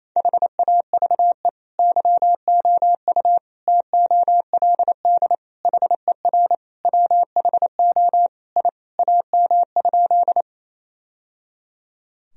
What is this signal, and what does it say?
Morse code, heard as a keyed tone, about 28 wpm, 700 Hz. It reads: HAVE YOU TOLD HER WHO I AM?
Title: Sentences from Top 1000 Words - Review Part 2 - 28wpm